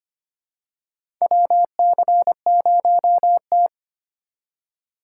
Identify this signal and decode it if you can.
Morse code, heard as a keyed tone, about 25 wpm, 700 Hz. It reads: WC0T
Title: USA Call Signs - 3x Character Spacing - 25wpm